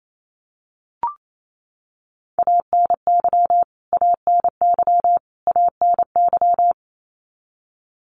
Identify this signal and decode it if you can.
Morse code, heard as a keyed tone, about 27 wpm, 700 Hz. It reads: ANY ANY ANY
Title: Top 300 Words Repeated 28wpm